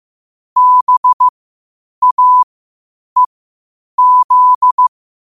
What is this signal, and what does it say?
Morse code, heard as a keyed tone, about 15 wpm, spaced as for 8 wpm, 1000 Hz. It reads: BAEZ
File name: qotdfiles / 8wpm